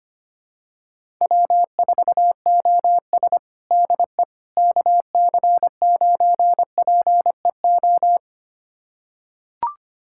Morse code, heard as a keyed tone, about 25 wpm, 700 Hz. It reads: W4OS DE KC9PEO
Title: USA Callsign DE Callsign 25wpm